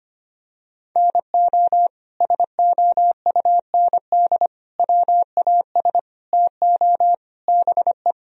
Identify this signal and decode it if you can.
Morse code, heard as a keyed tone, about 25 wpm, 700 Hz. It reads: NO SOUND WAS TO BE
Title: Sentences from Top 1000 Words - Review Part 1 - 25wpm